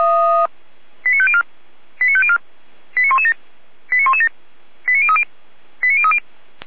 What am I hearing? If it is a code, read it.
Five-tone selective call (ZVEI), heard as quick runs of five tones, tones of about 70 ms. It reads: 89504 89504 89117 89117 89030 89030